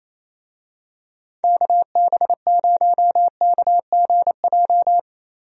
Morse code, heard as a keyed tone, about 28 wpm, 700 Hz. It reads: KB0KGJ
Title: USA Call Signs - 6 Characters - Repeated 3x - 28wpm